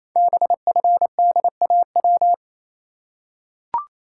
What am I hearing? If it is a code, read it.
Morse code, heard as a keyed tone, about 29 wpm, 700 Hz. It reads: BFDAW
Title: Code Groups - 5 Letters - 28wpm